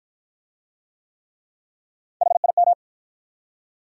Morse code, heard as a keyed tone, about 53 wpm, 700 Hz. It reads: HIM